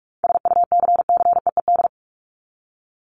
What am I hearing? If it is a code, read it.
Morse code, heard as a keyed tone, about 44 wpm, 700 Hz. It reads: SUCCEED
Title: Top 1501-1750 Words 45wpm